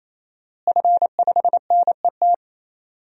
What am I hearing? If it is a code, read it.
Morse code, heard as a keyed tone, about 27 wpm, 700 Hz. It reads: F5NET
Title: DXCC-100 Call Signs - 5 Characters - Repeated 3x - 28wpm